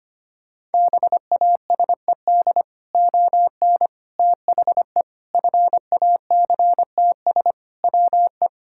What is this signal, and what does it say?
Morse code, heard as a keyed tone, about 25 wpm, 700 Hz. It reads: BASED ON THE FACTS WE
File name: practice / 25wpm / Sentences with 3-Word Phrases - 25wpm